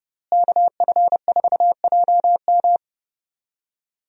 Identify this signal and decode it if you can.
Morse code, heard as a keyed tone, about 30 wpm, 700 Hz. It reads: KF4JM
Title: USA Call Signs - 5 Characters - Repeated 3x - 30wpm